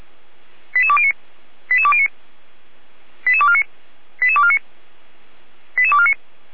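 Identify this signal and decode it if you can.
Five-tone selective call (ZVEI), heard as quick runs of five tones, tones of about 80 ms. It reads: 88209 88209 88249 88249 88249
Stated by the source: ZVEI five-tone selective call received by SDR.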